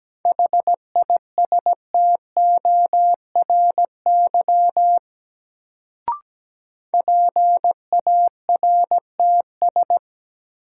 Morse code, heard as a keyed tone, about 17 wpm, 700 Hz. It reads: HISTORY PARTS
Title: Sets of 3 Words - New Word Drill - 17wpm